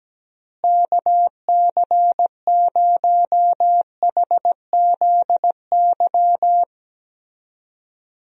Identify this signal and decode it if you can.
Morse code, heard as a keyed tone, about 17 wpm, 700 Hz. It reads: KC0HZY
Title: USA Call Signs - 6 Characters - Repeated 2x - 17wpm